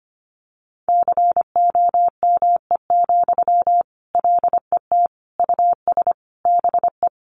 Morse code, heard as a keyed tone, about 25 wpm, 700 Hz. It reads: COME, LET US BE
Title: Sentences from Top 300 Words 25wpm